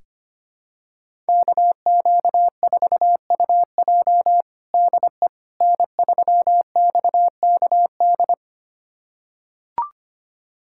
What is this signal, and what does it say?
Morse code, heard as a keyed tone, about 25 wpm, 700 Hz. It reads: KQ4UJ DE N3XKD
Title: USA Callsign DE Callsign - 5 Characters - 25wpm